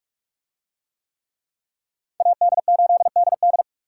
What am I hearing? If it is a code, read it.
Morse code, heard as a keyed tone, about 46 wpm, 700 Hz. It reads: AD8DD